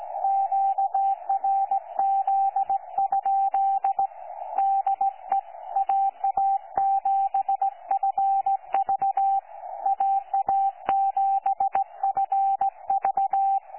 Morse code, heard as a keyed tone, about 17 wpm, 780 Hz. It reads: QRZ? DE AA7FV AA7FV
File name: 12 45Z 18088.1kHz